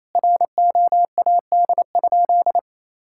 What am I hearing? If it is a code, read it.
Morse code, heard as a keyed tone, about 28 wpm, 700 Hz. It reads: ROAD?
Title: Sentences from Top 1000 Words - Review Part 5 - 28wpm